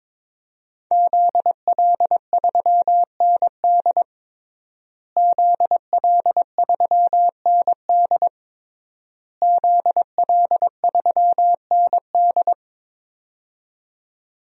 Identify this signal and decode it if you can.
Morse code, heard as a keyed tone, about 22 wpm, 700 Hz. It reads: ZL3ND ZL3ND ZL3ND
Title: DXCC-100 Call Signs - 5 Characters - Repeated 3x - 22wpm